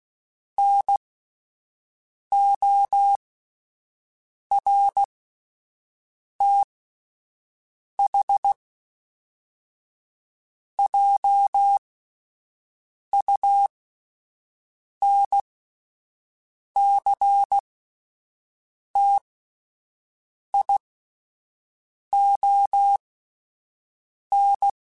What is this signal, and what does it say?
Morse code, heard as a keyed tone, about 16 wpm, spaced as for 5 wpm, 800 Hz. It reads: NORTHJUNCTION